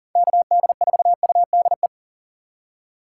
Morse code, heard as a keyed tone, about 40 wpm, 700 Hz. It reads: KD4UDE
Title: USA Call Signs - 3x Character Spacing - 40wpm